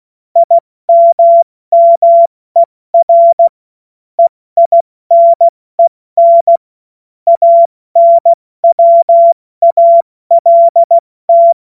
Morse code, heard as a keyed tone, about 16 wpm, spaced as for 14 wpm, 680 Hz. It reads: IMMER EINEN ANWALT
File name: Morsekurs / Witze / Witz 20. September 2022 70 F80